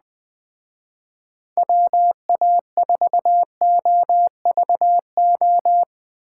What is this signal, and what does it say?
Morse code, heard as a keyed tone, about 19 wpm, 700 Hz. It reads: WA4OVO